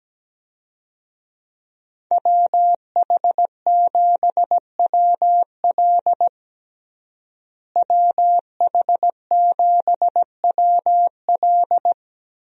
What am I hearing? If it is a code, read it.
Morse code, heard as a keyed tone, about 17 wpm, 700 Hz. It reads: WH7WL WH7WL
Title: USA Call Signs - 5 Characters - Repeated 3x - 17wpm